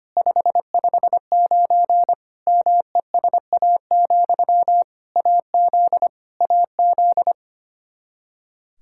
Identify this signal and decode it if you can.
Morse code, heard as a keyed tone, about 25 wpm, 700 Hz. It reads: 559 MESA, AZ AZ
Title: Practice QSO - Exchange 1 - Beginning - 25wpm